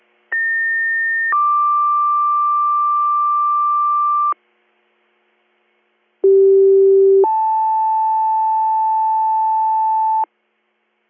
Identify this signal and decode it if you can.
Two-tone page, heard as a one-second tone, then a three-second tone, about 1800 then 1180 Hz, about 390 then 880 Hz. It is heard from 0.3 to 10.2 s.